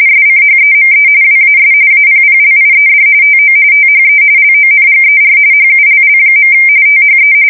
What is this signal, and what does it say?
RTTY, about 45.45 baud. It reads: SCHEME WAS APPLIED TO THE LONGER HULL. THE
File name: fsk170 / battleshipnj